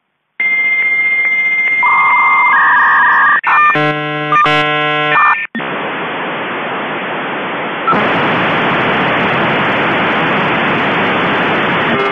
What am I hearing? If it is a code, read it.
Dial-up modem handshake.
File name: tones / 56K Modem